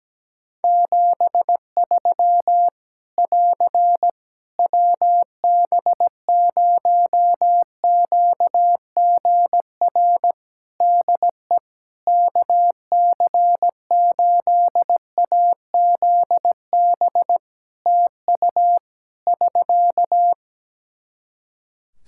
Morse code, heard as a keyed tone, about 17 wpm, 700 Hz. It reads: 73 <AR> WB0QGR DE KC8AZB TU <SK>
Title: Practice QSO - Exchange 4 - Ending - 17wpm